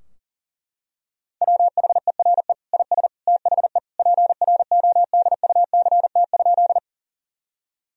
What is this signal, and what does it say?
Morse code, heard as a keyed tone, about 40 wpm, 700 Hz. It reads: WHERE IS THE PRODUCT?
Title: Sentences from Top 1000 Words - Review Part 2 - 40wpm